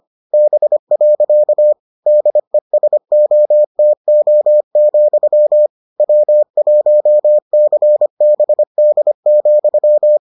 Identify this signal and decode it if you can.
Morse code, heard as a keyed tone, about 25 wpm, 600 Hz. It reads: B. DESOTO, W1CBD,